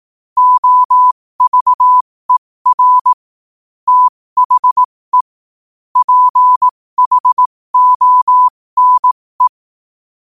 Morse code, heard as a keyed tone, about 18 wpm, spaced as for 15 wpm, 1000 Hz. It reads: OVER THE PHONE
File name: qotdfiles / 15wpm